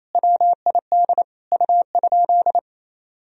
Morse code, heard as a keyed tone, about 28 wpm, 700 Hz. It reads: WID U?
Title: Sentences from Top 500 Words and Abbreviations 28wpm